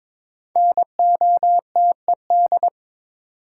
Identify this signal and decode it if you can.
Morse code, heard as a keyed tone, about 21 wpm, 700 Hz. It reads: NOTED
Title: Sets of 5 Words - New Word Drill - 22wpm